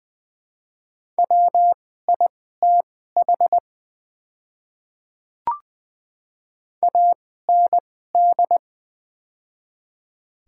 Morse code, heard as a keyed tone, about 21 wpm, spaced as for 15 wpm, 700 Hz. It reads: WITH AND
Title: 2x Character Spacing - Top 100 Words - 20wpm